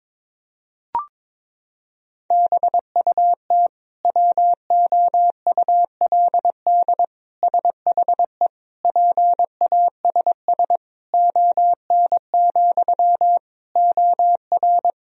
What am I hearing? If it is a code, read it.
Morse code, heard as a keyed tone, about 22 wpm, 700 Hz. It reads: BUT WOULD SHE PASS ON, OR
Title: Sentences from Top 1000 Words - Review Part 1 - 22wpm